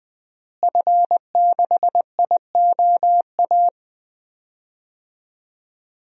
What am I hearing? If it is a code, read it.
Morse code, heard as a keyed tone, about 20 wpm, 700 Hz. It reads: F6IOA
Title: DXCC-100 Call Signs - 5 Characters - Repeated 3x - 20wpm